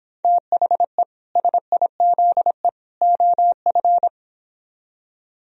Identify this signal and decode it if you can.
Morse code, heard as a keyed tone, about 26 wpm, 700 Hz. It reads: THE SIZE OF